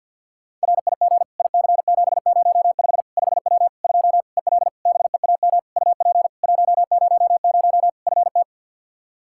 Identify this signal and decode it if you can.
Morse code, heard as a keyed tone, about 50 wpm, 700 Hz. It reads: RIG IC705 5W 2 EL BEAM UP 100 FT
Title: K1USN SST QSO Practice 50wpm